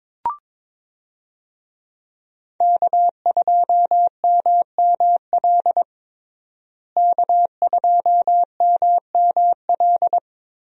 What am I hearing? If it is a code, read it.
Morse code, heard as a keyed tone, about 22 wpm, 700 Hz. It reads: K2MML K2MML
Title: USA Call Signs - 5 Characters - Repeated 3x - 22wpm